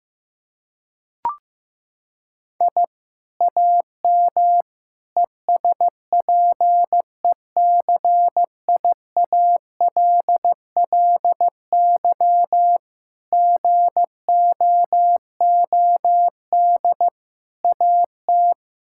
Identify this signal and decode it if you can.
Morse code, heard as a keyed tone, about 15 wpm, 700 Hz. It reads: I AM ESPECIALLY GOOD AT
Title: Sentences from Top 1000 Words - Review Part 6 - 15wpm